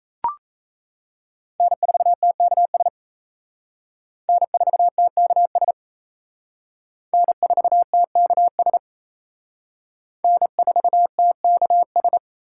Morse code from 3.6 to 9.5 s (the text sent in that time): N4TKS N4TKS